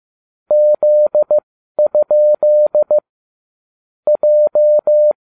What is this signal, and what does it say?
Morse code, heard as a keyed tone, about 15 wpm, spaced as for 12 wpm, 600 Hz. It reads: Z? J